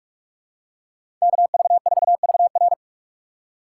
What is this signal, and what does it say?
Morse code, heard as a keyed tone, about 43 wpm, 700 Hz. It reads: KV4VR
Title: USA Call Signs - 5 Characters - 6x Character Spacing - 45wpm